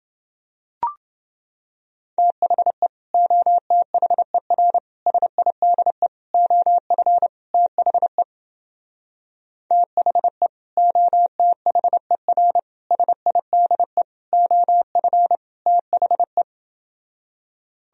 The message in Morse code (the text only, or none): THE OTHER SIDE OF THE THE OTHER SIDE OF THE